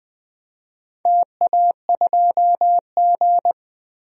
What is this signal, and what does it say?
Morse code, heard as a keyed tone, about 20 wpm, 700 Hz. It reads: TA2G
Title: DXCC-100 Call Signs - 4 Characters - 20wpm